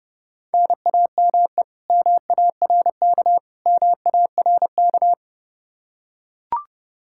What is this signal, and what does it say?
Morse code, heard as a keyed tone, about 32 wpm, 700 Hz. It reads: NAME MARK MARK
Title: Practice QSO - Exchange 1 - Beginning - 30wpm